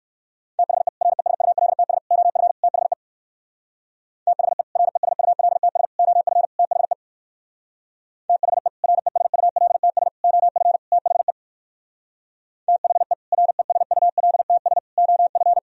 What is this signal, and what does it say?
Morse code, heard as a keyed tone, about 60 wpm, 700 Hz. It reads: THE RESULTS OF THE THE RESULTS OF THE THE RESULTS OF THE THE RESULTS OF